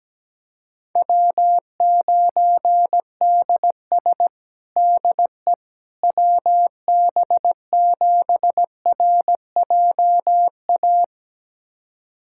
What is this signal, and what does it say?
Morse code, heard as a keyed tone, about 17 wpm, 700 Hz. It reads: W9DS DE WB7RJA
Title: USA Callsign DE Callsign 17wpm